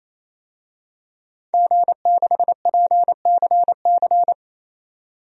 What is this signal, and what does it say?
Morse code, heard as a keyed tone, about 28 wpm, 700 Hz. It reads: G6PCC
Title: DXCC-100 Call Signs - 5 Characters - 28wpm